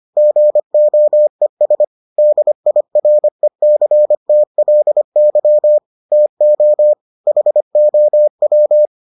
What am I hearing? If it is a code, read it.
Morse code, heard as a keyed tone, about 25 wpm, 600 Hz. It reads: GOES DIRECTLY TO HOW